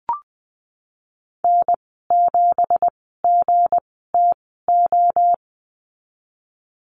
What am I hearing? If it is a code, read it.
Morse code, heard as a keyed tone, about 20 wpm, spaced as for 15 wpm, 700 Hz. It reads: N7GTO
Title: 193 Lesson 31g - Letter v - Call Signs 20wpm